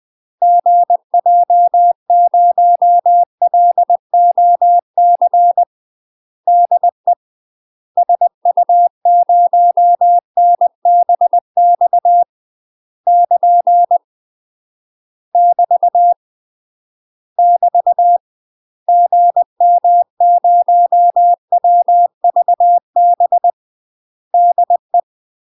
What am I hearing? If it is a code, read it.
Morse code, heard as a keyed tone, about 20 wpm, 700 Hz. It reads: GJ0LOC DE SU0NBX <KN> <BT> <BT> GM0WVB DE
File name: UK MASTER QSOs TRAINING / 20WPM / UK MASTER QSOs 0059